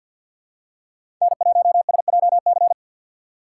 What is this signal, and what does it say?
Morse code, heard as a keyed tone, about 49 wpm, 700 Hz. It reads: N1SJC
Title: USA Call Signs - 5 Characters - 4x Character Spacing - 50wpm